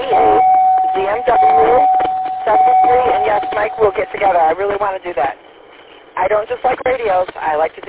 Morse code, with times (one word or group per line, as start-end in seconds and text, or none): none